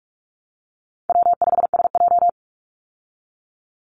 Morse code, heard as a keyed tone, about 44 wpm, 700 Hz. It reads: W5SJ
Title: Call Signs Repeated 3x - 45wpm